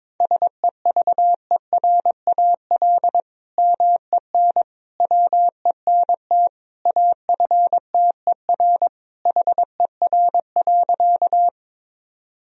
Morse code, heard as a keyed tone, about 22 wpm, 700 Hz. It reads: SEVERAL MEN WENT AFTER HER.
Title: Sentences from Top 1000 Words - Review Part 2 - 22wpm